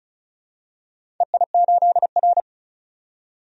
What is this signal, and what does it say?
Morse code, heard as a keyed tone, about 36 wpm, 700 Hz. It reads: EI8R
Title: DXCC-100 Call Signs - 4 Characters - 35wpm